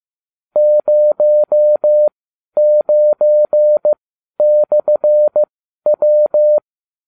Morse code, heard as a keyed tone, about 15 wpm, spaced as for 11 wpm, 600 Hz. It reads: 09/W